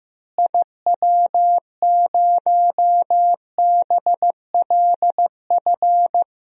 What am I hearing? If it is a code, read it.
Morse code, heard as a keyed tone, about 15 wpm, 700 Hz. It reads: IW0BLF
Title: DXCC-100 Call Signs - 6 Characters - Repeated 3x - 15wpm